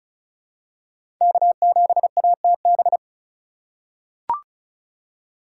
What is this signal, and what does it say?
Morse code, heard as a keyed tone, about 34 wpm, 700 Hz. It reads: K7ATB
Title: USA Call Signs - 5 Characters - 4x Character Spacing - 35wpm